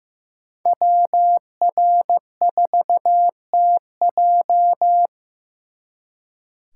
Morse code, heard as a keyed tone, about 18 wpm, 700 Hz. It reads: WR4TJ
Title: Speed Racing - USA 2x2 Call Signs 15wpm